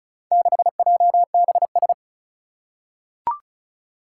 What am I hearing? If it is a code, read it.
Morse code, heard as a keyed tone, about 36 wpm, 700 Hz. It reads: 6JBS
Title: Four Challenging Letter-Number Combinations 35wpm